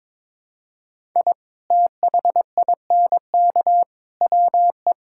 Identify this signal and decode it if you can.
Morse code, heard as a keyed tone, about 21 wpm, 700 Hz. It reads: I THINK WE
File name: practice / 22wpm / Build A Phrase - 5-Word Phrases - 22wpm